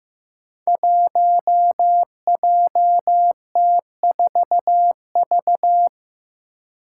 Morse code, heard as a keyed tone, about 15 wpm, 700 Hz. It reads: 1JT4V
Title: Five Challenging Letter-Number Combinations 15wpm